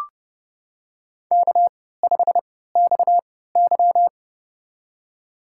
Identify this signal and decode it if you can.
Morse code, heard as a keyed tone, about 32 wpm, spaced as for 17 wpm, 700 Hz. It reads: K5XY
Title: USA Call Signs - 4 Characters - 3x Character Spacing - 30wpm